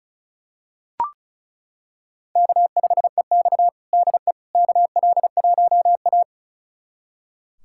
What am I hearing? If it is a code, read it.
Morse code, heard as a keyed tone, about 36 wpm, 700 Hz. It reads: K5EX DE KL1A
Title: USA Callsign DE Callsign - 4 Characters - 35wpm